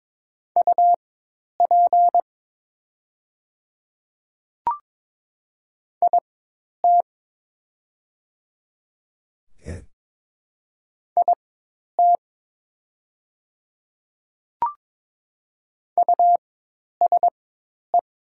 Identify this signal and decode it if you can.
Morse code, heard as a keyed tone, about 22 wpm, spaced as for 10 wpm, 700 Hz. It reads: UP IT IT USE
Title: 4x Character Spacing - Top 100 Words - 22wpm